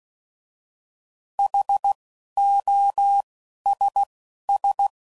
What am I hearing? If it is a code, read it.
Morse code, heard as a keyed tone, about 16 wpm, spaced as for 11 wpm, 800 Hz. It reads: HOSS